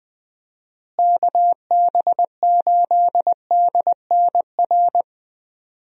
Morse code, heard as a keyed tone, about 21 wpm, 700 Hz. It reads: KB8DNR
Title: USA Call Signs - 6 Characters - Repeated 2x - 20wpm